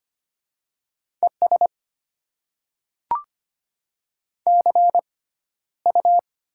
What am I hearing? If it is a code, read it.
Morse code, heard as a keyed tone, about 25 wpm, spaced as for 13 wpm, 700 Hz. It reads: ES CU
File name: Top 20 QSO Elements - 6x Character Spacing - 25wpm